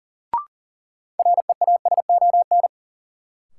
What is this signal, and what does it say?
Morse code, heard as a keyed tone, about 40 wpm, 700 Hz. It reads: REASON
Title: Sentences from Top 700 Words - New Word Drill - 40wpm